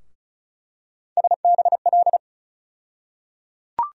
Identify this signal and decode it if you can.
Morse code, heard as a keyed tone, about 36 wpm, 700 Hz. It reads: SBL
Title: USA Call Sign Postfixes - 3 Letters 35wpm